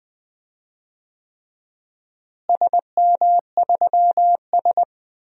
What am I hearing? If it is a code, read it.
Morse code, heard as a keyed tone, about 20 wpm, 700 Hz. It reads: SM3S
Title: DXCC-100 Call Signs - 4 Characters - Repeated 3x - 20wpm